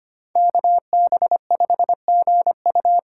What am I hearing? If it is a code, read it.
Morse code, heard as a keyed tone, about 25 wpm, 700 Hz. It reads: KB5GU
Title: USA Call Signs - 5 Characters - 3x Character Spacing - 25wpm